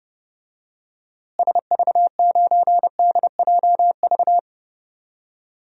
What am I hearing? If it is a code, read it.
Morse code, heard as a keyed tone, about 31 wpm, 700 Hz. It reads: SV9DJV
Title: DXCC-100 Call Signs - 6 Characters - Repeated 2x - 30wpm